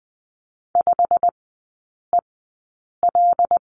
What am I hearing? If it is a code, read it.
Morse code, heard as a keyed tone, about 20 wpm, spaced as for 8 wpm, 700 Hz. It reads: 5EL
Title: 100 Lesson 17e - Letter u - Three Characters 20wpm